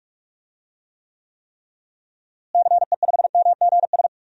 Morse code, heard as a keyed tone, about 46 wpm, 700 Hz. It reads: CE5MGS
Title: DXCC-100 Call Signs - 6 Characters - Repeated 3x - 45wpm